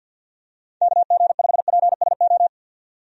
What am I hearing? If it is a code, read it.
Morse code, heard as a keyed tone, about 51 wpm, 700 Hz. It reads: KG5PSO
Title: USA Call Signs - 6 Characters - 50wpm